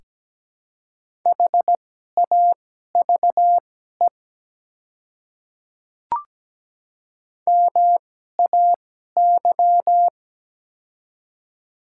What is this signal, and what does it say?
Morse code, heard as a keyed tone, about 17 wpm, spaced as for 12 wpm, 700 Hz. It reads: HAVE MAY
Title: 2x Character Spacing - Top 100 Words - 17wpm